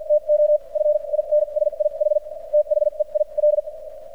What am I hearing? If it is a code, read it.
Morse code, heard as a keyed tone, about 50 wpm, spaced as for 39 wpm, 600 Hz. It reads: TO FINSIH THEIL
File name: archived / CRSnet